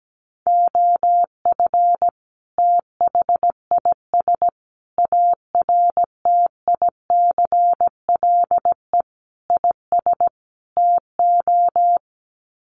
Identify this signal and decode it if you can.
Morse code, heard as a keyed tone, about 17 wpm, 700 Hz. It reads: OF THIS ARTICLE IS TO
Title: Sets of 5 Words Encore2 17wpm